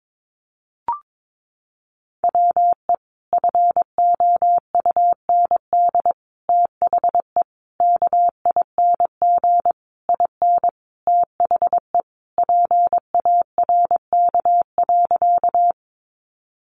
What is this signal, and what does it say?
Morse code, heard as a keyed tone, about 22 wpm, 700 Hz. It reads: WE FOUND THE KING IN THE PARK.
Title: Sentences from Top 1300 Words 22wpm